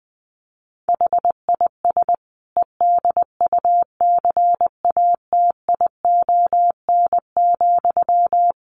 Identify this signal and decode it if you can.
Morse code, heard as a keyed tone, about 20 wpm, 700 Hz. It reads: HIS EDUCATION,
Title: Sentences from Top 1400 Words 20wpm